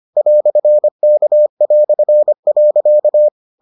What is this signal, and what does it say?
Morse code, heard as a keyed tone, about 25 wpm, 600 Hz. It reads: "K".